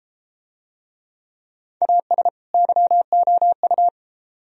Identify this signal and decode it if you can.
Morse code, heard as a keyed tone, about 33 wpm, 700 Hz. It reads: AS YOU